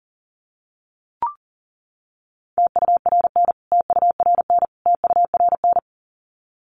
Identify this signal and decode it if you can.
Morse code, heard as a keyed tone, about 40 wpm, 700 Hz. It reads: TURN TURN TURN
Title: Top 500 Words Repeated 40wpm